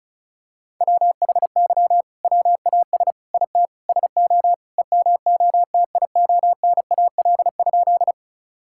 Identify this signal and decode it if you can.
Morse code, heard as a keyed tone, about 34 wpm, 700 Hz. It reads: WHY WAS IT SO EMOTIONAL?
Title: Sentences from Top 1000 Words - Review Part 5 - 35wpm